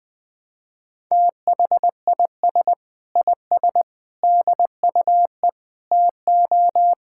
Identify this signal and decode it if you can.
Morse code, heard as a keyed tone, about 20 wpm, 700 Hz. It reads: THIS IS DUE TO
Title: Sets of 8 Words - 20wpm